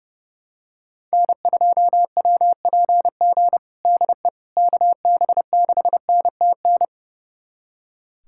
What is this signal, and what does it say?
Morse code, heard as a keyed tone, about 32 wpm, 700 Hz. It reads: N2WPG DE KB6NTN
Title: USA Callsign DE Callsign 30wpm